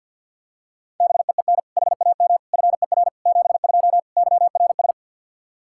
Morse code, heard as a keyed tone, about 52 wpm, 700 Hz. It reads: BEEN HAM FER 72 YRS
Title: Practice QSO - Exchange 3 - More Info - 50wpm